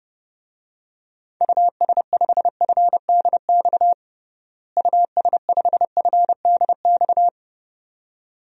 Morse code, heard as a keyed tone, about 30 wpm, 700 Hz. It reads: US5FDX US5FDX
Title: DXCC-100 Call Signs - 6 Characters - Repeated 2x - 30wpm